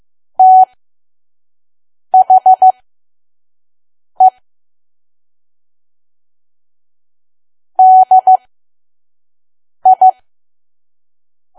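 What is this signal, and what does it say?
Morse code, heard as a keyed tone, about 15 wpm, spaced as for 5 wpm, 750 Hz. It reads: THE DI
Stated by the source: Morse code practice at 5 WPM from KQ2P.COM.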